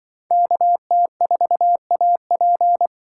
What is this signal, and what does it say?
Morse code, heard as a keyed tone, about 25 wpm, 700 Hz. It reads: KT4AP
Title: Speed Racing - USA 2x2 Call Signs 20wpm